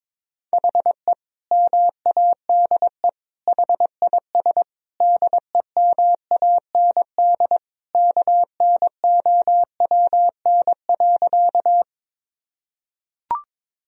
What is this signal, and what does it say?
Morse code, heard as a keyed tone, about 22 wpm, 700 Hz. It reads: HE MADE HIS DEMAND KNOWN.
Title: Sentences from Top 1000 Words - Review Part 4 - 22wpm